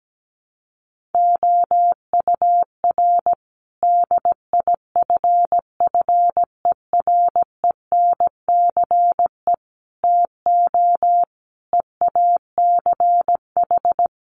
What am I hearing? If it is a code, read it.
Morse code, heard as a keyed tone, about 17 wpm, 700 Hz. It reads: OUR DIFFERENCE TO EACH
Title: Sentences from Top 1500 Words 17wpm